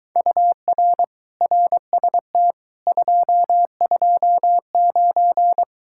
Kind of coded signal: Morse code

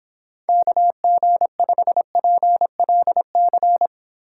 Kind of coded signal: Morse code

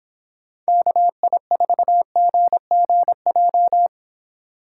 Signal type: Morse code

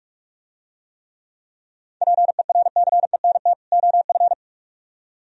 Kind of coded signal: Morse code

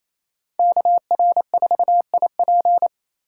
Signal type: Morse code

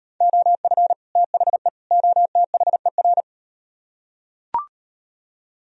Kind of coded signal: Morse code

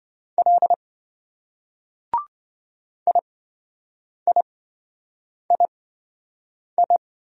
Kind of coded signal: Morse code

